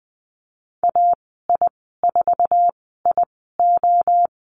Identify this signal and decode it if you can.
Morse code, heard as a keyed tone, about 19 wpm, spaced as for 14 wpm, 700 Hz. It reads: AI4IO
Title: 128 Lesson 21g - Number 6 - Call Signs 20wpm